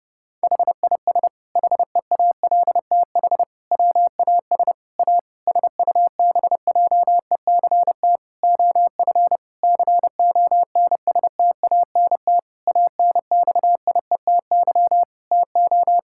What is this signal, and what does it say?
Morse code, heard as a keyed tone, about 30 wpm, 700 Hz. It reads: HIS HEALTH WAS A SUBJECT OF CONSTANT ANXIETY TO
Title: Sentences from Top 1600 Words 30wpm